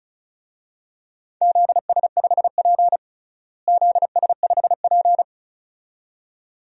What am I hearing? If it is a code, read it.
Morse code, heard as a keyed tone, about 36 wpm, 700 Hz. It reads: ZS5P ZS5P
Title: DXCC-100 Call Signs - 4 Characters - Repeated 2x - 35wpm